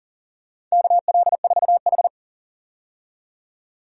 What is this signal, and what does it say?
Morse code, heard as a keyed tone, about 40 wpm, 700 Hz. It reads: KL4H
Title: USA Call Signs - 4 Characters - 40wpm